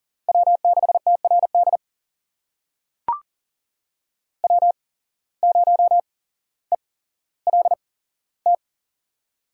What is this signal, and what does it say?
Morse code, heard as a keyed tone, about 40 wpm, spaced as for 17 wpm, 700 Hz. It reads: W6TRD W0ELT
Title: USA Call Signs - 8x Character Spacing - 40wpm